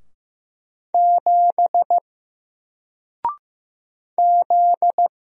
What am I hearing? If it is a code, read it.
Morse code, heard as a keyed tone, about 15 wpm, 700 Hz. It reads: 7 Z
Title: Single Letter-Number 15wpm